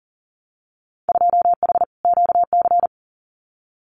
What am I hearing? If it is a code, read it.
Morse code, heard as a keyed tone, about 40 wpm, 700 Hz. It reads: 2H QC
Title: Winter Field Day Exchanges 40wpm